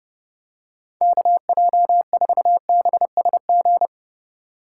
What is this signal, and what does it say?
Morse code, heard as a keyed tone, about 30 wpm, 700 Hz. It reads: KJ4BSG
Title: USA Call Signs - 6 Characters - 3x Character Spacing - 30wpm